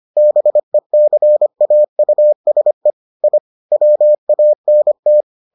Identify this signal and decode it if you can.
Morse code, heard as a keyed tone, about 25 wpm, 600 Hz. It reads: BECAUSE I WANT